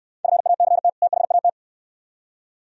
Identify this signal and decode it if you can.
Morse code, heard as a keyed tone, about 68 wpm, 700 Hz. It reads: FACT THAT